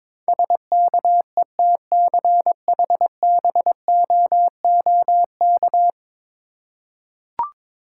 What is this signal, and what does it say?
Morse code, heard as a keyed tone, about 21 wpm, 700 Hz. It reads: SKETCHBOOK
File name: Difficult Words with Hints - 22wpm